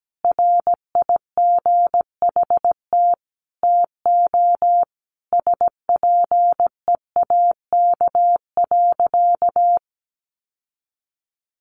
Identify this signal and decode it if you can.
Morse code, heard as a keyed tone, about 17 wpm, 700 Hz. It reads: RIGHT TO SPEAK.
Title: Sentences from Top 1300 Words 17wpm